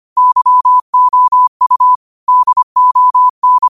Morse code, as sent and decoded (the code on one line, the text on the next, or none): -.-- --- ..- / -.. --- -.
YOU DON